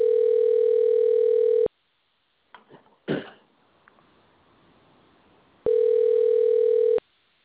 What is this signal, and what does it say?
Ringback tone (North America), 440 + 480 Hz.
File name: recordings / REcbf357b4ad95b88fe59c76708d3952ff